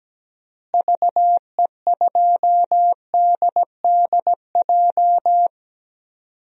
Morse code, heard as a keyed tone, about 17 wpm, 700 Hz. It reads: VE2DDJ
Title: DXCC-100 Call Signs - 6 Characters - Repeated 3x - 17wpm